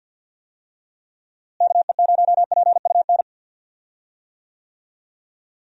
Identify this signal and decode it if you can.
Morse code, heard as a keyed tone, about 50 wpm, 700 Hz. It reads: KE0PUN
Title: USA Call Signs - 6 Characters - Repeated 2x - 50wpm